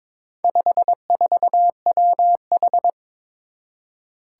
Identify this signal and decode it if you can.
Morse code, heard as a keyed tone, about 22 wpm, 700 Hz. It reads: 54WH
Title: Four Challenging Letter-Number Combinations 22wpm